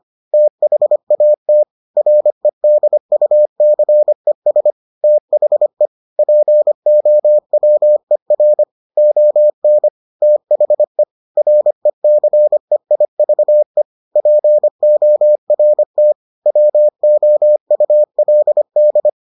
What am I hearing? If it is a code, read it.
Morse code, heard as a keyed tone, about 25 wpm, 600 Hz. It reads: THAT REDUCES THE POWER ON THE RECEIVE PORT WOULD